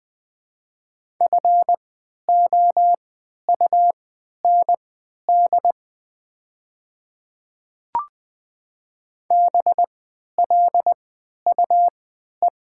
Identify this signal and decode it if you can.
Morse code, heard as a keyed tone, about 20 wpm, spaced as for 11 wpm, 700 Hz. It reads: FOUND BLUE
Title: 3x Character Spacing - Top 500 Words - 20wpm